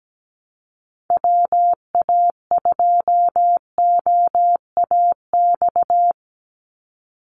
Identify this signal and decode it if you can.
Morse code, heard as a keyed tone, about 17 wpm, 700 Hz. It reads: WA2OAX
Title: Call Signs Repeated 2x - 17wpm